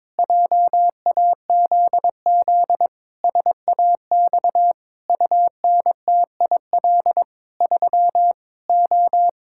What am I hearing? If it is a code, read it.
Morse code, heard as a keyed tone, about 21 wpm, 700 Hz. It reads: JAZZ SAX UNTIL 3 O
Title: Pangram Challenge v2 - 22wpm